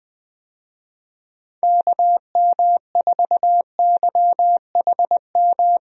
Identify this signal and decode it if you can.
Morse code, heard as a keyed tone, about 20 wpm, 700 Hz. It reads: KM4YHM
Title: USA Call Signs - 6 Characters - Repeated 3x - 20wpm